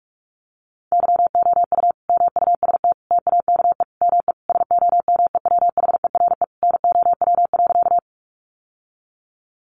Morse code, heard as a keyed tone, about 44 wpm, 700 Hz. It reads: YOU MUST TAKE ME SOMEWHERE NOW.
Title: Sentences from Top 1400 Words 45wpm